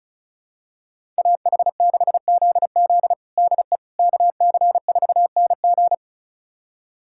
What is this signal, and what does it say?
Morse code, heard as a keyed tone, about 36 wpm, 700 Hz. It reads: AH6ZZ DE KC4NG
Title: USA Callsign DE Callsign - 5 Characters - 35wpm